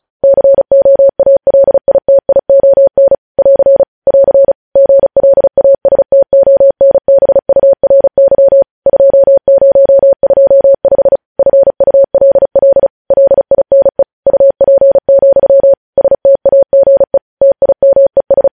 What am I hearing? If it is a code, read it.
Morse code, heard as a keyed tone, about 36 wpm, 550 Hz. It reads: COALITION <AR> <AR> GLASTONBURY 2025 FULL LINE UP, STAGE TIMES
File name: podcast / News headlines-1x-35x35